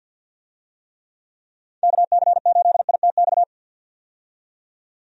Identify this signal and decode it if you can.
Morse code, heard as a keyed tone, about 48 wpm, 700 Hz. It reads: KC8ITX
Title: USA Call Signs - 6 Characters - 50wpm